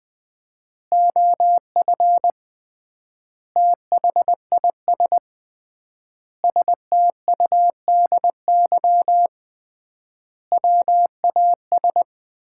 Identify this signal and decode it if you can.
Morse code, heard as a keyed tone, about 21 wpm, 700 Hz. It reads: OF THIS STUDY WAS
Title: 3x Word-Spacing - Sets of 5 Words - 20wpm